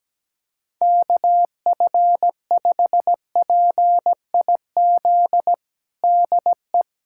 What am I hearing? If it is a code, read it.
Morse code, heard as a keyed tone, about 17 wpm, 700 Hz. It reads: KF5PIZ DE
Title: USA Callsign DE Callsign 17wpm